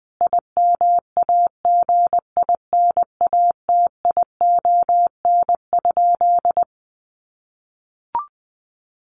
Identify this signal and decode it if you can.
Morse code, heard as a keyed tone, about 20 wpm, 700 Hz. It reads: IMAGINATION?
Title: Sentences from Top 1300 Words 20wpm